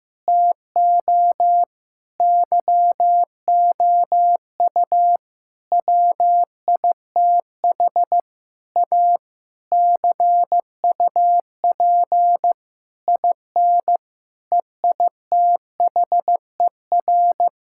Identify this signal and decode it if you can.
Morse code, heard as a keyed tone, about 15 wpm, 700 Hz. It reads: TO YOU WITH A CUP IN EITHER
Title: Sentences from Top 1600 Words 15wpm